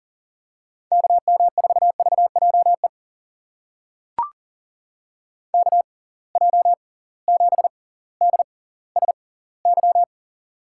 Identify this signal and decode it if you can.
Morse code, heard as a keyed tone, about 40 wpm, spaced as for 20 wpm, 700 Hz. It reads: KM4VJE KJ7DSY
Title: USA Call Signs - 6 Characters - 6x Character Spacing - 40wpm